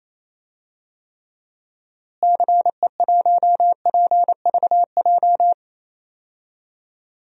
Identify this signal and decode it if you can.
Morse code, heard as a keyed tone, about 28 wpm, 700 Hz. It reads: CE1PVJ
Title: DXCC-100 Call Signs - 6 Characters - Repeated 3x - 28wpm